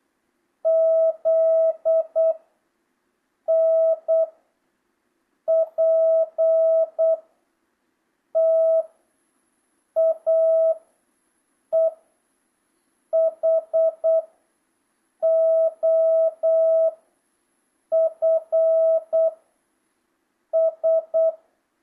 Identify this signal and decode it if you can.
Morse code, heard as a keyed tone, about 8 wpm, 650 Hz. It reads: Z N P T A E H O F S
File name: jh8cbh / 2025cw